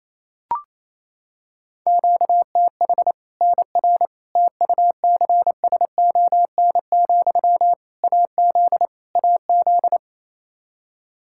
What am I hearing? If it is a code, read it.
Morse code, heard as a keyed tone, about 28 wpm, 700 Hz. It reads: QTH NR TUCSON, AZ AZ
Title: Practice QSO - Exchange 1 - Beginning - 28wpm